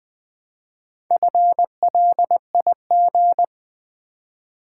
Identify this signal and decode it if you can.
Morse code, heard as a keyed tone, about 20 wpm, 700 Hz. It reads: FLIG
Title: Build A Word - 2 Letters At A Time - 6-Letter Words - 20wpm